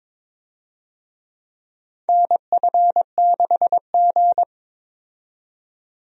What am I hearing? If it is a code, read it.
Morse code, heard as a keyed tone, about 23 wpm, 700 Hz. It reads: NF6G
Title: USA Call Signs - 4 Characters - 22wpm